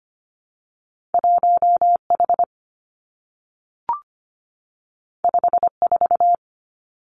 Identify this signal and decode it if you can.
Morse code, heard as a keyed tone, about 25 wpm, 700 Hz. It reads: 1H 54
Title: Two Challenging Letter-Number Combinations 25wpm